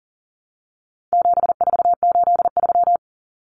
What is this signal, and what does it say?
Morse code, heard as a keyed tone, about 37 wpm, 700 Hz. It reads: 7483
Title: Common Cut Numbers - 4-Digit 40wpm